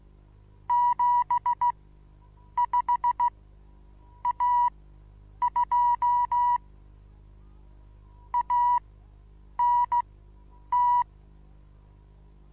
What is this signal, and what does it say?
Morse code, heard as a keyed tone, about 16 wpm, spaced as for 8 wpm, 970 Hz. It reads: 75A2 ANT
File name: Onlinelehrgang / morsen / DJ4UF Morsen25